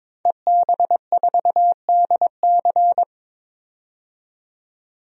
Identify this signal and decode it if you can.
Morse code, heard as a keyed tone, about 22 wpm, 700 Hz. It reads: EB4DC
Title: DXCC-100 Call Signs - 5 Characters - 22wpm